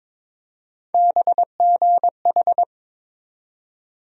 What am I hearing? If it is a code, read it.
Morse code, heard as a keyed tone, about 21 wpm, 700 Hz. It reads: BGH
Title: USA Call Sign Postfixes - 3 Letters 22wpm